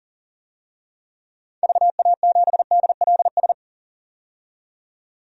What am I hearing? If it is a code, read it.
Morse code, heard as a keyed tone, about 40 wpm, 700 Hz. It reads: VA7DLS
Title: DXCC-100 Call Signs - 6 Characters - 40wpm